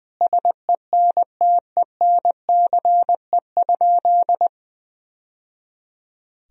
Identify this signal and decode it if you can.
Morse code, heard as a keyed tone, about 21 wpm, 700 Hz. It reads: SENTENCE?
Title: Sentences from Top 1000 Words - Review Part 1 - 20wpm